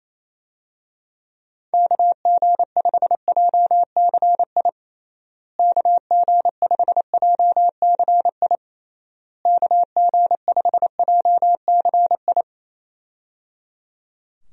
Morse code, heard as a keyed tone, about 28 wpm, 700 Hz. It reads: KG5JCI KG5JCI KG5JCI
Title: USA Call Signs - 6 Characters - Repeated 3x - 28wpm